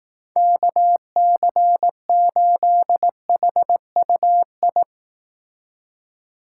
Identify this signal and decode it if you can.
Morse code, heard as a keyed tone, about 18 wpm, 700 Hz. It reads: KC8HUI
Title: Speed Racing - USA 2x3 Call Signs 15wpm